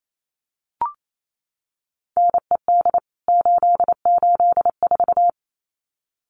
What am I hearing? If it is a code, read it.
Morse code, heard as a keyed tone, about 28 wpm, 700 Hz. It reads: NED 884
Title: CWT Exchange 28wpm